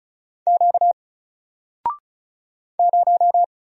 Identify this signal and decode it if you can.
Morse code, heard as a keyed tone, about 36 wpm, 700 Hz. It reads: Q 0
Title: Single Letter-Number Alt 35wpm